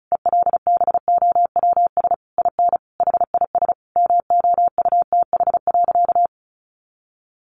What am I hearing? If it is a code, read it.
Morse code, heard as a keyed tone, about 34 wpm, 700 Hz. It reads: ELBOWS IN HIS MOUTH.
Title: A Sound Of Thunder 35wpm